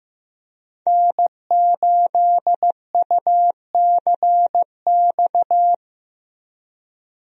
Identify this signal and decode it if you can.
Morse code, heard as a keyed tone, about 15 wpm, 700 Hz. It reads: N8UCX
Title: USA Call Signs - 5 Characters - 15wpm